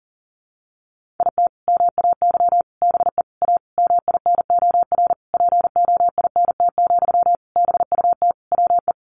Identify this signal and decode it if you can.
Morse code, heard as a keyed tone, about 40 wpm, 700 Hz. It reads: IT MAY BE A MINOR POINT, BUT WE
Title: Sentences from Top 1500 Words 40wpm